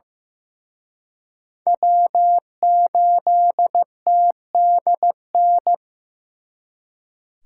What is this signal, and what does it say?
Morse code, heard as a keyed tone, about 18 wpm, 700 Hz. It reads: W8TDN
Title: Speed Racing - USA 1x3 Call Signs 15wpm